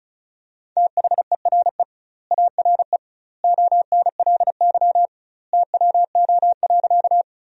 Morse code, heard as a keyed tone, about 36 wpm, 700 Hz. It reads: THERE ARE ONLY TWO.
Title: 2x Word-Spacing - Sentences from Top 200 Words - 35wpm